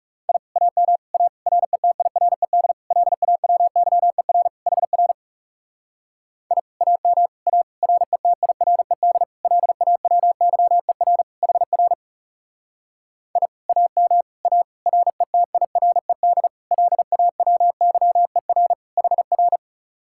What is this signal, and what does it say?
Morse code, heard as a keyed tone, about 38 wpm, 700 Hz. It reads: I AM A RETIRED LAWYER HR I AM A RETIRED LAWYER HR I AM A RETIRED LAWYER HR